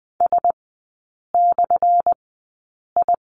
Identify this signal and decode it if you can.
Morse code, heard as a keyed tone, about 20 wpm, spaced as for 8 wpm, 700 Hz. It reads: S/I
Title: 218 Lesson 35e - Number 0 - Three Characters 20wpm